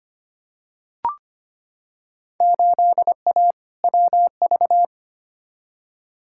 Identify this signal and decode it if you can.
Morse code, heard as a keyed tone, about 25 wpm, 700 Hz. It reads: 8A WV
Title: Field Day Exchanges - 25wpm